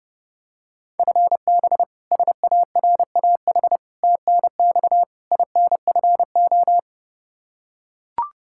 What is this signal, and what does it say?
Morse code, heard as a keyed tone, about 32 wpm, 700 Hz. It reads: FB SARAH TNX INFO
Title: K1USN SST QSO Practice 30wpm